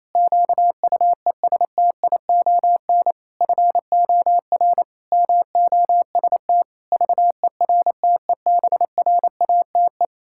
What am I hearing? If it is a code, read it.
Morse code, heard as a keyed tone, about 28 wpm, 700 Hz. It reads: QUESTION FOR MOST VERTEBRATE
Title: Pangram Challenge v2 - 28wpm